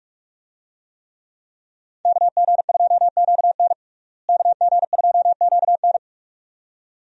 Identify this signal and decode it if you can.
Morse code, heard as a keyed tone, about 41 wpm, 700 Hz. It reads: KG2QN KG2QN
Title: USA Call Signs - 5 Characters - Repeated 2x - 45wpm